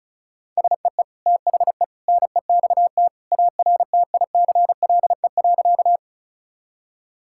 Morse code, heard as a keyed tone, about 34 wpm, 700 Hz. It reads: SEE THE NEXT ARTICLE.
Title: Sentences from Top 1000 Words - Review Part 5 - 35wpm